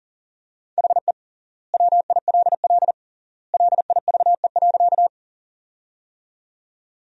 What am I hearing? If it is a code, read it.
Morse code, heard as a keyed tone, about 40 wpm, 700 Hz. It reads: HE WILL LIVE.
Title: 3x Word-Spacing - Sentences from Top 200 Words - 40wpm